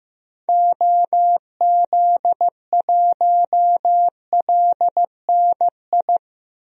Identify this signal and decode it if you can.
Morse code, heard as a keyed tone, about 15 wpm, 700 Hz. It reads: OZ1LNI
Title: DXCC-100 Call Signs - 6 Characters - Repeated 3x - 15wpm